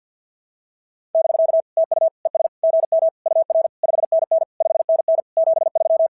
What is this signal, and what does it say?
Morse code, heard as a keyed tone, about 45 wpm, 660 Hz. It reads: <BK> TU ES GM UR 5NN 5NN 73
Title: SOTA QSO Practice 50wpm